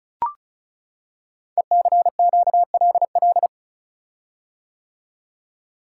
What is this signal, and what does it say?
Morse code, heard as a keyed tone, about 36 wpm, 690 Hz. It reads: ECQLL
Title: Code Groups - 5 Letters - 35wpm